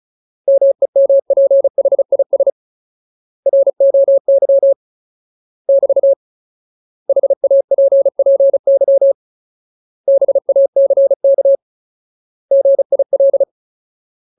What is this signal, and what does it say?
Morse code, heard as a keyed tone, about 34 wpm, 550 Hz. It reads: MEMPHIS ROY <BT> HAPPY BACK GIL
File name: Ultimate Word List / 35WPM Ultimate